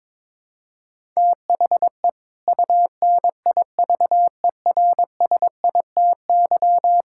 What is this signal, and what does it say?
Morse code, heard as a keyed tone, about 22 wpm, 700 Hz. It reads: THE UNIVERSITY
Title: Sentences from Top 1000 Words - Review Part 5 - 22wpm